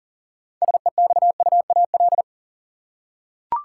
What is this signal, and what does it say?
Morse code, heard as a keyed tone, about 40 wpm, 700 Hz. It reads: SEXUAL
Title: Six Letter Words - 40wpm